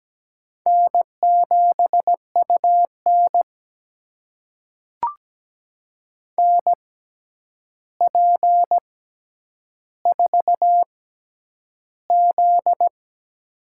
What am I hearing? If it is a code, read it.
Morse code, heard as a keyed tone, about 17 wpm, spaced as for 9 wpm, 700 Hz. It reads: N7UN NP4Z
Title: USA Call Signs - 4 Characters - 6x Character Spacing - 17wpm